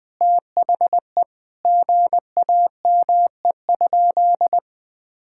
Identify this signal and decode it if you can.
Morse code, heard as a keyed tone, about 20 wpm, 700 Hz. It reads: THE GAME?
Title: Sentences from Top 1600 Words 20wpm